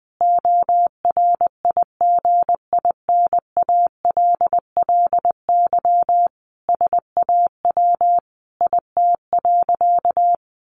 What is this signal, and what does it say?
Morse code, heard as a keyed tone, about 20 wpm, 700 Hz. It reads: ORIGINALLY SAW IT.
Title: Sentences from Top 1500 Words 20wpm